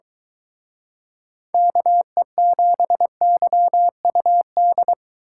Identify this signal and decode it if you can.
Morse code, heard as a keyed tone, about 22 wpm, 700 Hz. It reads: KE7YUD